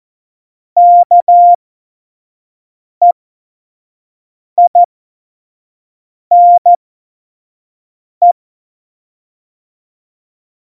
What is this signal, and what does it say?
Morse code, heard as a keyed tone, about 14 wpm, spaced as for 5 wpm, 700 Hz. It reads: KEINE
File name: Morsekurs / Witze / Witz 24.Dezember 2019, 25 F70